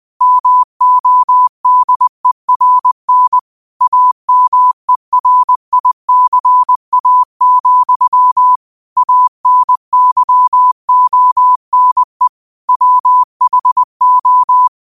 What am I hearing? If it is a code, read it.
Morse code, heard as a keyed tone, about 20 wpm, 1000 Hz. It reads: MODERN AMERICA, ANYONE WHO
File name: 20wpm